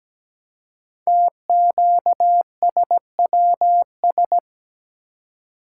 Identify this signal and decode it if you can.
Morse code, heard as a keyed tone, about 17 wpm, 700 Hz. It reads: TQSWS
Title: Code Groups - 5 Letters - 17wpm